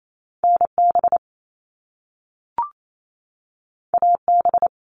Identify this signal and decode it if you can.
Morse code, heard as a keyed tone, about 28 wpm, 700 Hz. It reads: NB AB
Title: Canadian Provinces and Territories 28wpm